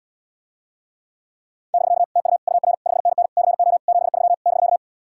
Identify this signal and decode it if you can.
Morse code, heard as a keyed tone, about 78 wpm, 700 Hz. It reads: <BK> TU UR 5NN CO 73 <BK>